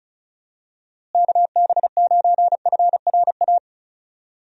Morse code, heard as a keyed tone, about 34 wpm, 700 Hz. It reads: KB9FRA
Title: USA Call Signs - 6 Characters - Repeated 3x - 35wpm